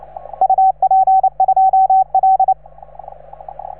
Morse code, heard as a keyed tone, about 28 wpm, 750 Hz. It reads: UP2L